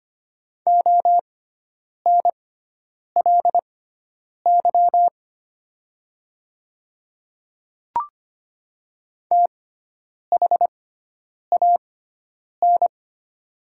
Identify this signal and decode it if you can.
Morse code, heard as a keyed tone, about 25 wpm, spaced as for 9 wpm, 690 Hz. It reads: ONLY THAN
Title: 6x Character Spacing - Top 200 Words - 25wpm